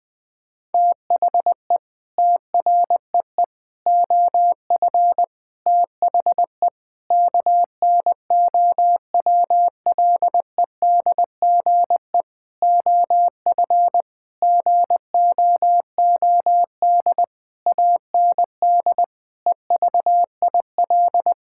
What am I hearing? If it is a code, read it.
Morse code, heard as a keyed tone, about 20 wpm, 700 Hz. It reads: THE TREE OF THE KNOWLEDGE OF GOOD AND EVIL
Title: Sets of 9 Words - 20wpm